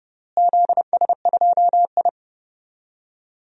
Morse code, heard as a keyed tone, about 30 wpm, 700 Hz. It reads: ZS2I
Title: DXCC-100 Call Signs - 4 Characters - Repeated 2x - 30wpm